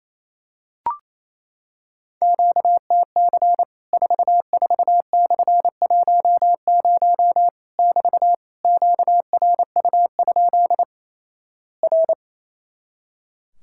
Morse code, heard as a keyed tone, about 28 wpm, 700 Hz. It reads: QTC 44/10 <BT> QRU? R
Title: Worked All Europe DX Contest - QTC Exchange - 28wpm